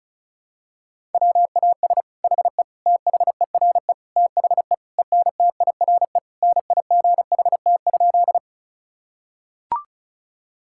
Morse code, heard as a keyed tone, about 36 wpm, 690 Hz. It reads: WAS HE THERE THE ENTIRE NIGHT?
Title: Sentences from Top 1000 Words - Review Part 5 - 35wpm